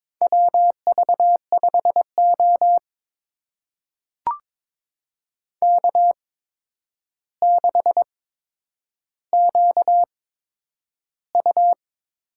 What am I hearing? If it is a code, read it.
Morse code, heard as a keyed tone, about 22 wpm, spaced as for 9 wpm, 700 Hz. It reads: WV5O K6QU
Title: USA Call Signs - 4 Characters - 8x Character Spacing - 22wpm